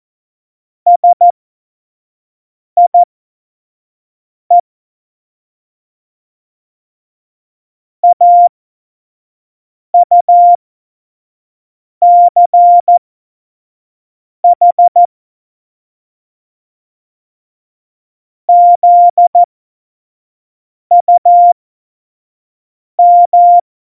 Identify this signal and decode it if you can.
Morse code, heard as a keyed tone, about 14 wpm, spaced as for 5 wpm, 700 Hz. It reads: SIE AUCH ZUM
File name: Morsekurs / Witze / Witz 06 March 2018, 25 F70